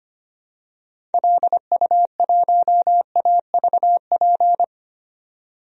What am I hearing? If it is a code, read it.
Morse code, heard as a keyed tone, about 25 wpm, 700 Hz. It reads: LU1AVP
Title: DXCC-100 Call Signs - 6 Characters - Repeated 3x - 25wpm